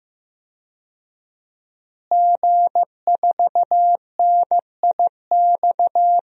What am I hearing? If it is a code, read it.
Morse code, heard as a keyed tone, about 15 wpm, 700 Hz. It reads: G4NIX
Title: DXCC-100 Call Signs - 5 Characters - Repeated 3x - 15wpm